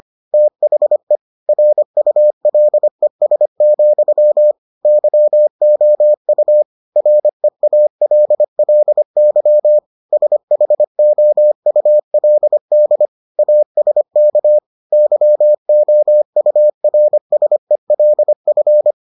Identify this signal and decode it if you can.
Morse code, heard as a keyed tone, about 25 wpm, 600 Hz. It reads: THE RULES, YOU REALLY SHOULD ASK YOURSELF